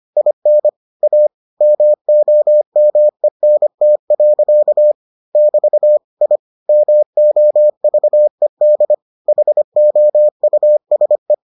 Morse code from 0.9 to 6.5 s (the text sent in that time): A MOMENT. <BT> I